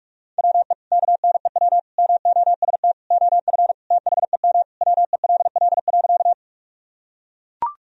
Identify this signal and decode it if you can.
Morse code, heard as a keyed tone, about 42 wpm, 700 Hz. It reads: WE KNEW MOST OF THEM WELL.
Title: Sentences from Top 1000 Words - Review Part 1 - 45wpm